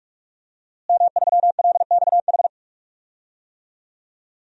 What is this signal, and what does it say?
Morse code, heard as a keyed tone, about 48 wpm, 700 Hz. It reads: M3LXH
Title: DXCC-100 Call Signs - 5 Characters - Repeated 2x - 45wpm